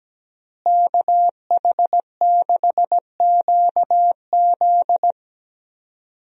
Morse code, heard as a keyed tone, about 17 wpm, 700 Hz. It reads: KH6QZ
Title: USA Call Signs - 5 Characters - Repeated 2x - 17wpm